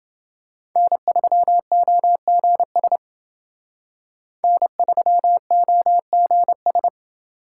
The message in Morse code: N3OGS N3OGS

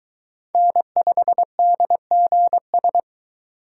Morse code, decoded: N5DGS

